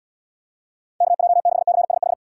Morse code, heard as a keyed tone, about 72 wpm, 700 Hz. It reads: KJ6CSH